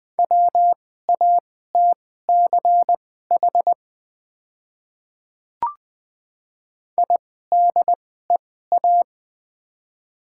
Morse code, heard as a keyed tone, about 20 wpm, spaced as for 15 wpm, 700 Hz. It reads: WATCH IDEA
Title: 2x Character Spacing - Top 300 Words - 20wpm